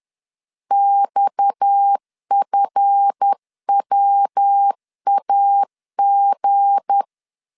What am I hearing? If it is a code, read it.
Morse code, heard as a keyed tone, about 10 wpm, 800 Hz. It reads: XFWAG
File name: CW / DET / 1 9 1500C S B 10wpm 4439